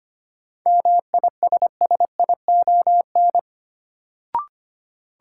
Morse code, heard as a keyed tone, about 25 wpm, 700 Hz. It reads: MISSION
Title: Sentences from Top 1600 Words - New Word Drill - 25wpm